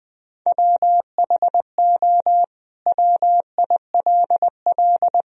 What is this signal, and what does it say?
Morse code, heard as a keyed tone, about 19 wpm, 700 Hz. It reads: WHO WILL
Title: Sentences from Top 1000 Words - Review Part 2 - 20wpm